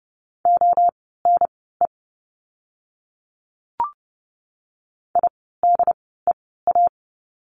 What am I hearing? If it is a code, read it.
Morse code, heard as a keyed tone, about 30 wpm, spaced as for 17 wpm, 700 Hz. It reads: ONE IDEA
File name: course / 30:10wpm / 108 Lesson 18f - Letter m - Words 30wpm